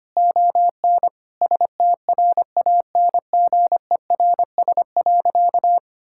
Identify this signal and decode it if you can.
Morse code, heard as a keyed tone, about 25 wpm, 700 Hz. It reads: ON STRANGERS.
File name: practice / 25wpm / Difficult Sentences with Hints - 25wpm